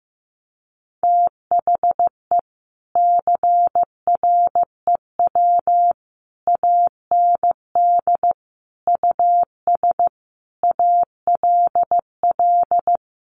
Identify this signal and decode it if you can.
Morse code, heard as a keyed tone, about 15 wpm, 700 Hz. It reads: THE CREW AND US ALL
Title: Sentences from Top 1500 Words 15wpm